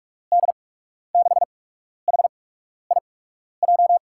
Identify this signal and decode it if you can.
Morse code, heard as a keyed tone, about 44 wpm, 700 Hz. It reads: D 6 H I J
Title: USA Call Signs - 8x Character Spacing - 45wpm